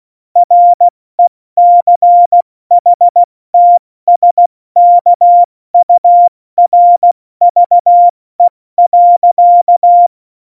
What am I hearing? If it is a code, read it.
Morse code, heard as a keyed tone, about 16 wpm, 700 Hz. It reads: RECHTSKURVE.
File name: Morsekurs / Witze / Witz 04 August 2020, 70 F80